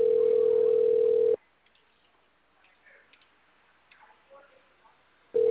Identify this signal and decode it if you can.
Ringback tone (North America), 440 + 480 Hz.
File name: out- -2051- - - .49747